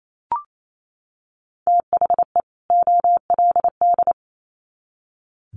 Morse code, heard as a keyed tone, about 28 wpm, 700 Hz. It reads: THE OLD
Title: Sets of 2 Words 28wpm